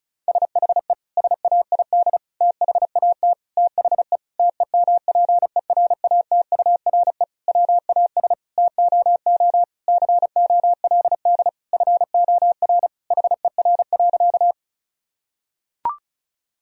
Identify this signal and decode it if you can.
Morse code, heard as a keyed tone, about 36 wpm, 700 Hz. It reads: SHE SAID THAT THE TEMPERATURE WAS TOO COLD FOR HER.
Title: Sentences from Top 1000 Words - Review Part 4 - 35wpm